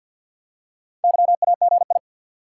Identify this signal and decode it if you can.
Morse code, heard as a keyed tone, about 50 wpm, 700 Hz. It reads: YAGI
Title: Top 180 QSO Elements 50wpm